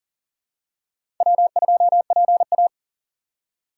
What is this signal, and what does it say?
Morse code, heard as a keyed tone, about 40 wpm, 700 Hz. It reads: W2PA
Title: USA Call Signs - 4 Characters - 4x Character Spacing - 40wpm